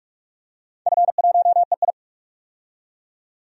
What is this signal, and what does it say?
Morse code, heard as a keyed tone, about 46 wpm, 700 Hz. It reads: F1EI
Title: DXCC-100 Call Signs - 4 Characters - 45wpm